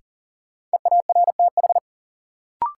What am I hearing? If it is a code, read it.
Morse code, heard as a keyed tone, about 40 wpm, 700 Hz. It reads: EARTH
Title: Sentences from Top 200 Words - New Word Drill - 40wpm